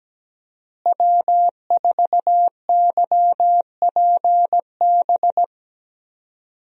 Morse code, heard as a keyed tone, about 17 wpm, 700 Hz. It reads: W4YPB
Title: USA Call Signs - 5 Characters - Repeated 2x - 17wpm